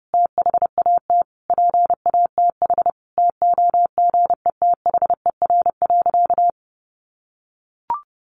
Morse code, heard as a keyed tone, about 28 wpm, 700 Hz. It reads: THAT PATH TOGETHER.
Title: Sentences from Top 1400 Words 30wpm